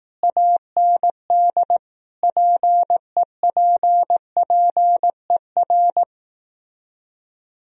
Morse code, heard as a keyed tone, about 18 wpm, 700 Hz. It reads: AND PEPPER
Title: Speed-Racing - Sets of 5 Words 15wpm